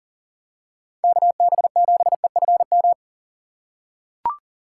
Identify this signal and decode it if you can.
Morse code, heard as a keyed tone, about 40 wpm, 700 Hz. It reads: KB7EFM
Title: USA Call Signs - 6x Character Spacing - 40wpm